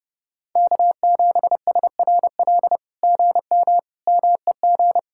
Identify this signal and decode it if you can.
Morse code, heard as a keyed tone, about 30 wpm, 700 Hz. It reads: K7SRL GM MEG
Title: SOTA QSO Practice 30wpm